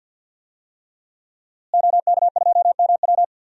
Morse code, heard as a keyed tone, about 51 wpm, 700 Hz. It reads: OK2MW
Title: DXCC-100 Call Signs - 5 Characters - Repeated 3x - 50wpm